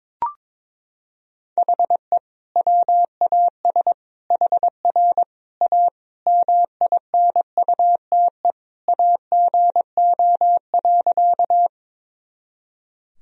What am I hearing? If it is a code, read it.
Morse code, heard as a keyed tone, about 22 wpm, 700 Hz. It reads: HE WAS HR A MINUTE AGO.
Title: Sentences from Top 500 Words and Abbreviations 22wpm